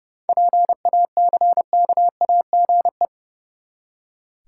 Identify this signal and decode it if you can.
Morse code, heard as a keyed tone, about 30 wpm, 700 Hz. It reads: PACKAGE
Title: Seven Letter Words - 30wpm